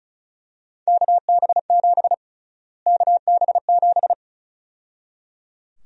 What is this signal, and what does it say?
Morse code, heard as a keyed tone, about 34 wpm, 700 Hz. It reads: KB7 KB7
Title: USA Call Sign Prefixes - Repeated 2x - 35wpm